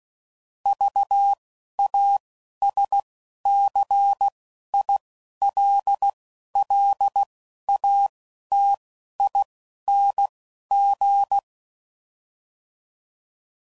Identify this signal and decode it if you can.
Morse code, heard as a keyed tone, about 16 wpm, spaced as for 11 wpm, 800 Hz. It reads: VASCILLATING
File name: Suffixes / Special INGs / Single INGs